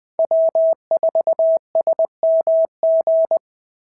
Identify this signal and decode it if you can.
Morse code, heard as a keyed tone, about 19 wpm, 690 Hz. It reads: W4SMG
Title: SOTA QSO Practice 20wpm